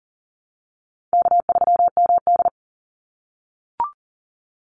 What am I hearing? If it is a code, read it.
Morse code, heard as a keyed tone, about 40 wpm, 700 Hz. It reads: K3MD
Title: Call Signs - 40wpm